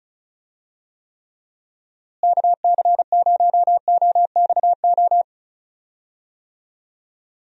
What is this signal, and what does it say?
Morse code, heard as a keyed tone, about 36 wpm, 700 Hz. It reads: KC0OXO
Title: USA Call Signs - 6 Characters - Repeated 2x - 35wpm